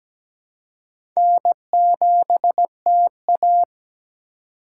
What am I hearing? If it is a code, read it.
Morse code, heard as a keyed tone, about 17 wpm, 700 Hz. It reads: N7TA
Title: USA Call Signs - 4x Character Spacing - 17wpm